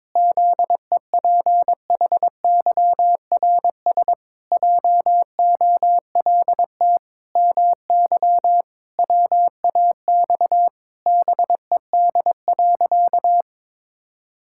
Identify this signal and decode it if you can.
Morse code, heard as a keyed tone, about 22 wpm, 700 Hz. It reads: ZEPHYRS JOLT MY WAX BED.
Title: Pangram Challenge v2 - 22wpm